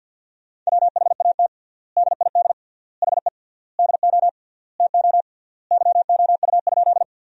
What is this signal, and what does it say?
Morse code, heard as a keyed tone, about 50 wpm, 700 Hz. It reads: WHAT DID HE DO TO YOU?
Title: 3x Word-Spacing - Sentences from Top 100 Words - 50wpm